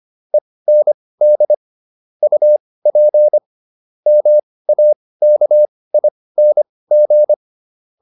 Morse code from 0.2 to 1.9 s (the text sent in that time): END